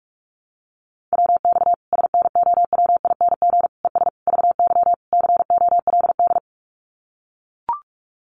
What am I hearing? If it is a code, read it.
Morse code, heard as a keyed tone, about 44 wpm, 700 Hz. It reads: WX SNOWING ES VY COLD
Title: Common QSO Phrases 45wpm